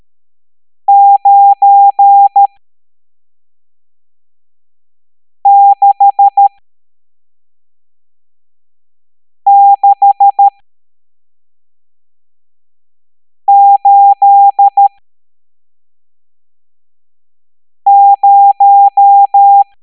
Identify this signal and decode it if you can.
Morse code, heard as a keyed tone, about 13 wpm, spaced as for 3 wpm, 800 Hz. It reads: 96680
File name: hamdata / L11-EX01-67890